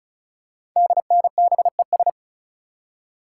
Morse code, heard as a keyed tone, about 36 wpm, 690 Hz. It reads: DNBES
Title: Code Groups - 5 Letters - 35wpm